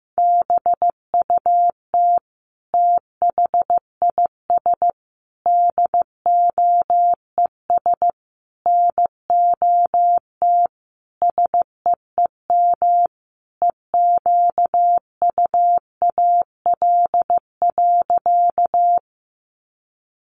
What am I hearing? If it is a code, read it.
Morse code, heard as a keyed tone, about 15 wpm, 700 Hz. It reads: BUT THIS DOES NOT SEEM EQUAL.
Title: Sentences from Top 1400 Words 15wpm